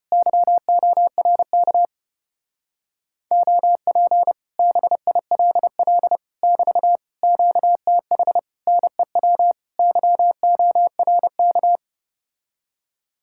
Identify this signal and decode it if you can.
Morse code, heard as a keyed tone, about 33 wpm, 700 Hz. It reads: YORK OP BILL <BT> QTH NEW YORK